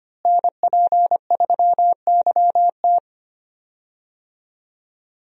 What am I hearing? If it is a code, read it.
Morse code, heard as a keyed tone, about 25 wpm, 700 Hz. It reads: NP3YT
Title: USA Call Signs - 5 Characters - Repeated 2x - 25wpm